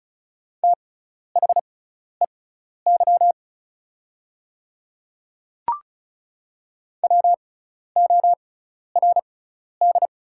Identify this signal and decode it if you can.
Morse code, heard as a keyed tone, about 34 wpm, spaced as for 12 wpm, 690 Hz. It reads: THEY WORD
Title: 6x Character Spacing - Top 100 Words - 35wpm